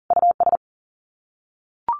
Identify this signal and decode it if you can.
Morse code, heard as a keyed tone, about 40 wpm, 700 Hz. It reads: US
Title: Sets of 5 Words 40wpm